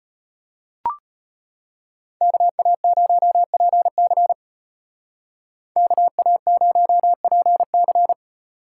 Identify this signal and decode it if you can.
Morse code, heard as a keyed tone, about 34 wpm, 700 Hz. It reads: KA0PC KA0PC